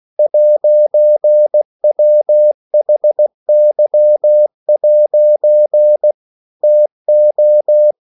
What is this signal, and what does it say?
Morse code, heard as a keyed tone, about 16 wpm, 600 Hz. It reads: 'WHY' TO
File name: CW / Practice / EnglishQuotations0216